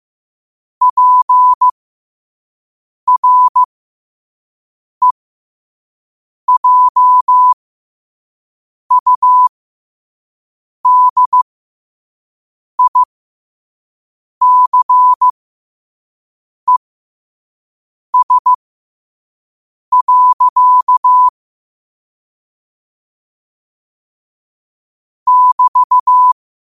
Morse code, heard as a keyed tone, about 15 wpm, spaced as for 5 wpm, 1000 Hz. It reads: PREJUDICES. <BT>
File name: qotdfiles / 5wpm